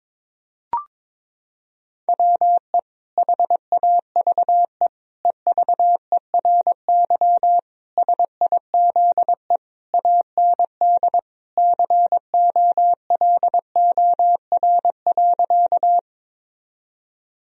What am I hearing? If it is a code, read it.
Morse code, heard as a keyed tone, about 22 wpm, 700 Hz. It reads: WE HAVE EVERY SIZE AND COLOR.
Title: Sentences from Top 1000 Words - Review Part 2 - 22wpm